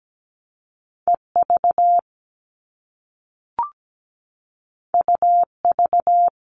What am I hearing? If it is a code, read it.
Morse code, heard as a keyed tone, about 17 wpm, 700 Hz. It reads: EV UV
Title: All Cut Numbers - 2-Digit 17wpm